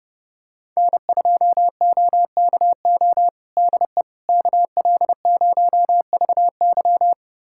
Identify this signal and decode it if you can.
Morse code, heard as a keyed tone, about 30 wpm, 700 Hz. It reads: N2OKO DE KL0VY
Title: USA Callsign DE Callsign 30wpm